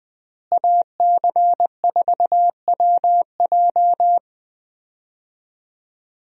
Morse code, heard as a keyed tone, about 20 wpm, 700 Hz. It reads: AC4WJ
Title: USA Call Signs - 5 Characters - Repeated 2x - 20wpm